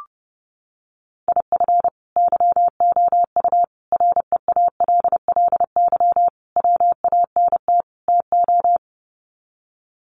Morse code, heard as a keyed tone, about 30 wpm, 700 Hz. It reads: IF YOU REALLY WANT TO
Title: Sets of 5 Words Encore2 30wpm